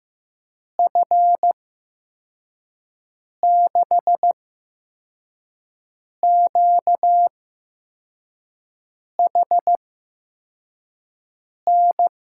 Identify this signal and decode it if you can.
Morse code, heard as a keyed tone, about 15 wpm, spaced as for 4 wpm, 700 Hz. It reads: F6QHN
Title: USA Call Signs - 6 Characters - 8x Character Spacing - 15wpm